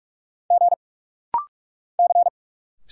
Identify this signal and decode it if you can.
Morse code, heard as a keyed tone, about 45 wpm, 690 Hz. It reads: G C
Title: Single Letters - Rapid-Fire 45wpm